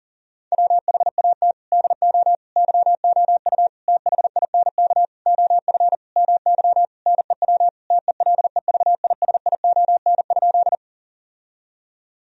Morse code, heard as a keyed tone, about 40 wpm, 700 Hz. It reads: WHAT DO YOU THINK OF MY NEW TELEVISION?
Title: Sentences from Top 1000 Words - Review Part 4 - 40wpm